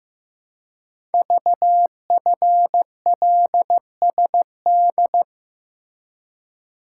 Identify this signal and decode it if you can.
Morse code, heard as a keyed tone, about 15 wpm, 700 Hz. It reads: VFLSD
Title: Code Groups - 5 Letters - 15wpm